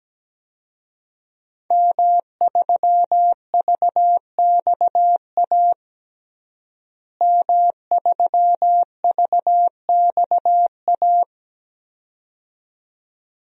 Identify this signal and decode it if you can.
Morse code, heard as a keyed tone, about 17 wpm, 700 Hz. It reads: M3VXA M3VXA
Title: DXCC-100 Call Signs - 5 Characters - Repeated 2x - 17wpm